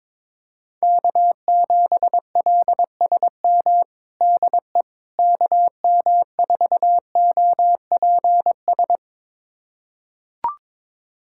Morse code, heard as a keyed tone, about 22 wpm, 700 Hz. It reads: K7LSM DE KM4OPS
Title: USA Callsign DE Callsign 22wpm